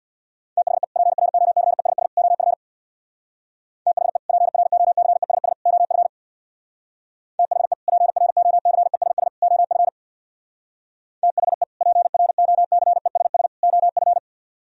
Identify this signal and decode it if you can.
Morse code, heard as a keyed tone, about 60 wpm, 700 Hz. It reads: THE PROCESS OF THE PROCESS OF THE PROCESS OF THE PROCESS OF
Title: Speed-Racing - Sets of 3 Words 50wpm